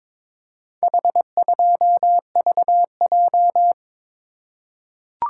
Morse code, heard as a keyed tone, about 21 wpm, 700 Hz. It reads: H2VJ
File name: practice / 22wpm / Four Letter-Number Combinations - 22wpm